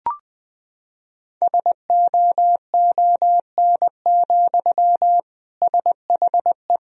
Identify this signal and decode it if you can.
Morse code, heard as a keyed tone, about 20 wpm, 700 Hz. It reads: SOON, SHE
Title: Sentences from Top 1000 Words - Review Part 1 - 20wpm